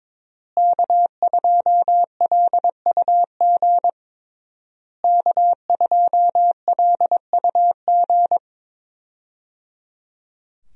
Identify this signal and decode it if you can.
Morse code, heard as a keyed tone, about 22 wpm, 700 Hz. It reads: K2LUG K2LUG
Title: USA Call Signs - 5 Characters - Repeated 3x - 22wpm